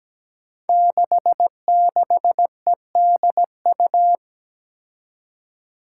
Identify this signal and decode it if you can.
Morse code, heard as a keyed tone, about 17 wpm, 700 Hz. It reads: 66EDU
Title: Five Challenging Letter-Number Combinations 17wpm